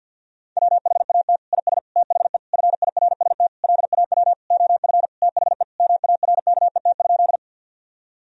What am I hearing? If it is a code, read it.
Morse code, heard as a keyed tone, about 51 wpm, 700 Hz. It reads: WHAT IS THE FIRST LAW OF THE MARKET?
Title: Sentences from Top 1000 Words - Review Part 3 - 50wpm